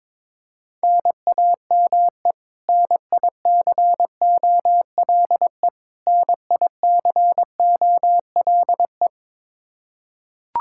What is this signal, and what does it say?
Morse code, heard as a keyed tone, about 22 wpm, 700 Hz. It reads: NAME NICOLE NICOLE
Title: Practice QSO - Exchange 1 - Beginning - 22wpm